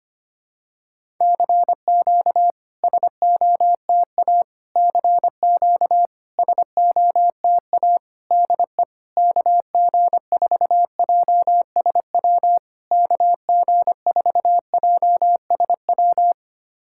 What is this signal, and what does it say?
Morse code, heard as a keyed tone, about 25 wpm, 700 Hz. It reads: CQ SOTA CQ SOTA DE KG4JSW KG4JSW
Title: SOTA QSO Practice 25wpm